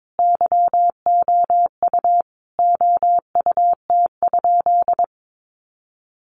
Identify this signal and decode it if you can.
Morse code, heard as a keyed tone, about 22 wpm, 700 Hz. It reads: YOU OUT?
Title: Sentences from Top 1300 Words 22wpm